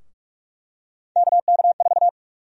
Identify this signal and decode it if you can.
Morse code, heard as a keyed tone, about 46 wpm, 700 Hz. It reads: KK4
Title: USA Call Sign Prefixes - 45wpm